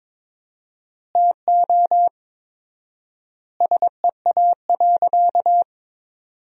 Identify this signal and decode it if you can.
Morse code, heard as a keyed tone, about 22 wpm, 700 Hz. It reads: TO SEA.
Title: 4x Word-Spacing - Sentences from Top 400 Words - 22wpm